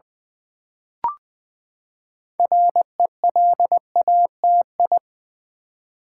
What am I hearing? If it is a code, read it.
Morse code, heard as a keyed tone, about 20 wpm, 700 Hz. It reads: RELATI
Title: Build A Word - 6 Letters At A Time - 12-Letter Words - 20wpm